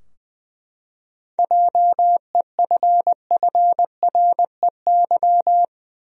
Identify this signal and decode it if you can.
Morse code, heard as a keyed tone, about 20 wpm, 700 Hz. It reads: JEFFREY
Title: USA Common Names - 20wpm